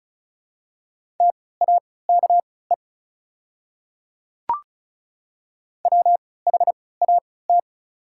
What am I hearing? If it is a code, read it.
Morse code, heard as a keyed tone, about 36 wpm, spaced as for 20 wpm, 700 Hz. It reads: TAKE WHAT
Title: 3x Character Spacing - Top 300 Words - 35wpm